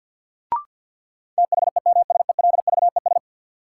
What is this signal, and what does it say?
Morse code, heard as a keyed tone, about 50 wpm, 700 Hz. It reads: THEMSELVES
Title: Sentences from Top 800 Words - New Word Drill - 50wpm